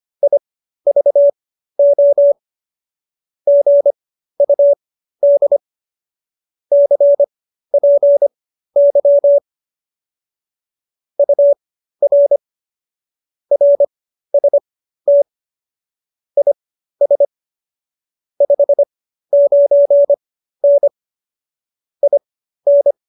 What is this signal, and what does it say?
Morse code, heard as a keyed tone, about 25 wpm, spaced as for 13 wpm, 570 Hz. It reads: IVO GUD CPY UR RST IS 59N IN